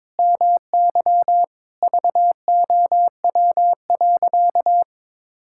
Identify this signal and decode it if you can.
Morse code, heard as a keyed tone, about 21 wpm, 700 Hz. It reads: MY VOW.
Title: Pangram Challenge v2 - 22wpm